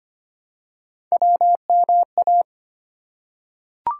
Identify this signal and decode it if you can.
Morse code, heard as a keyed tone, about 25 wpm, 700 Hz. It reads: WMA
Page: morse-code-ninja-practice-25wpm